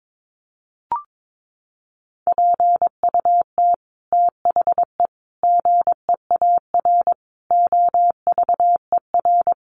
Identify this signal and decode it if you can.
Morse code, heard as a keyed tone, about 22 wpm, 700 Hz. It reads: PUT THE GEAR OVER
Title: Sentences from Top 1400 Words 22wpm